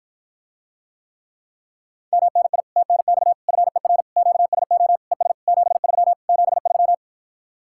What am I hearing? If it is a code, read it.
Morse code, heard as a keyed tone, about 52 wpm, 700 Hz. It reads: MNI TNX FER QSO ES 73 73